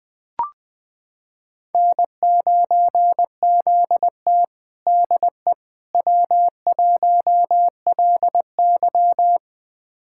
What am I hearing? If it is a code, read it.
Morse code, heard as a keyed tone, about 20 wpm, 700 Hz. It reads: N9ZT DE W1LY
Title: USA Callsign DE Callsign - 4 Characters - 20wpm